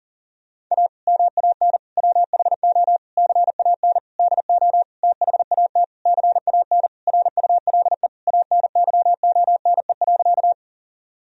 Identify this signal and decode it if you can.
Morse code, heard as a keyed tone, about 40 wpm, 700 Hz. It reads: A MAN WHO CAN DO THAT CAN RULE ANYONE.
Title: Sentences from Top 1000 Words - Review Part 2 - 40wpm